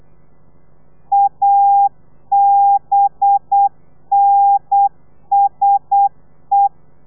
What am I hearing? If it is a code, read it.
Morse code, heard as a keyed tone, about 8 wpm, 800 Hz. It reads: ABNSE
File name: telegrafia / i2tfj / Lez4 35